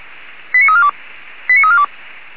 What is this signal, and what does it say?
Five-tone selective call (ZVEI), heard as quick runs of five tones, tones of about 70 ms. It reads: 89342 89342